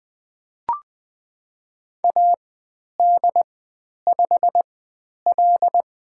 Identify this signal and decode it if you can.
Morse code, heard as a keyed tone, about 20 wpm, spaced as for 10 wpm, 700 Hz. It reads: AD5L
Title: 102 Lesson 17g - Letter u - Call Signs 20wpm